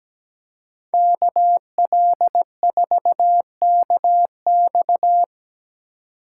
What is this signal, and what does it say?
Morse code, heard as a keyed tone, about 19 wpm, 700 Hz. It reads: KL4KX
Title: Speed Racing - USA Call Signs 17wpm